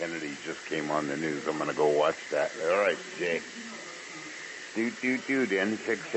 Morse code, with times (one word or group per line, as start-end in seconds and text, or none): none